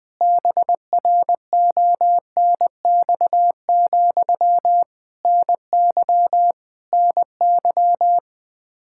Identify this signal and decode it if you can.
Morse code, heard as a keyed tone, about 20 wpm, 700 Hz. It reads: BRONX, NY NY
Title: Practice QSO - Exchange 1 - Beginning - 20wpm